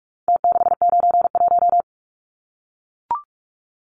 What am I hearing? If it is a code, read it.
Morse code, heard as a keyed tone, about 44 wpm, 700 Hz. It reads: T691
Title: Numbers - Contest Style - 4-Digit 45wpm